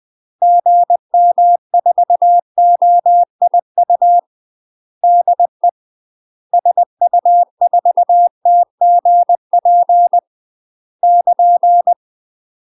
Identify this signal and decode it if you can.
Morse code, heard as a keyed tone, about 20 wpm, 700 Hz. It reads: GM4OIU DE SU4TGP <KN>
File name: UK MASTER QSOs TRAINING / 20WPM / UK MASTER QSOs 0011